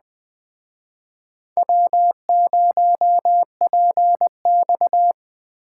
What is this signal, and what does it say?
Morse code, heard as a keyed tone, about 21 wpm, 700 Hz. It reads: W0PX
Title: Speed Racing - USA 1x2 Call Signs 17wpm